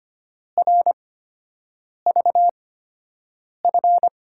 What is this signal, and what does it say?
Morse code, heard as a keyed tone, about 25 wpm, spaced as for 7 wpm, 700 Hz. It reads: RVF
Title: USA Call Signs - 6 Characters - 8x Character Spacing - 25wpm